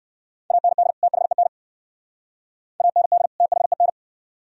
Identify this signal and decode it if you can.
Morse code, heard as a keyed tone, about 59 wpm, 690 Hz. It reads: AND THEN AND THEN